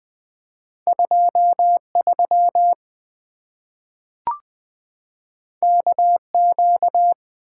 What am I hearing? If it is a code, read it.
Morse code, heard as a keyed tone, about 20 wpm, 700 Hz. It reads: 23 KQ
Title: Two Letter-Number Combinations - 20wpm